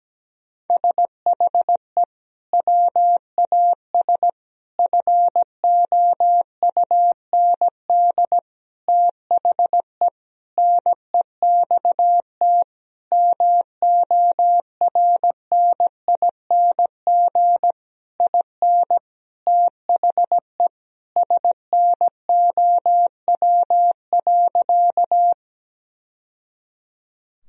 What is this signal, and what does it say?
Morse code, heard as a keyed tone, about 17 wpm, 700 Hz. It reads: SHE WAS FOUND THE NEXT MORNING IN THE SNOW.
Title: Sentences from Top 1000 Words - Review Part 3 - 17wpm